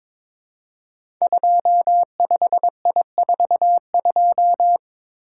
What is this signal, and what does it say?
Morse code, heard as a keyed tone, about 22 wpm, 700 Hz. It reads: 25I42
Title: Code Groups - 5 Letter:Numbers - 22wpm